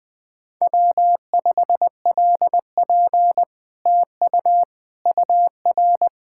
Morse code, heard as a keyed tone, about 20 wpm, 700 Hz. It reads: W5LP TU UR
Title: Speed-Racing Common QSO Phrases - 20wpm